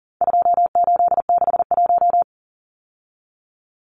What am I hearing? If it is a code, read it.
Morse code, heard as a keyed tone, about 40 wpm, 700 Hz. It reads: 2861
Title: Numbers - 4-Digit 40wpm